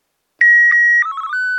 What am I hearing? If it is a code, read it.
SSTV (slow-scan television), Martin 1.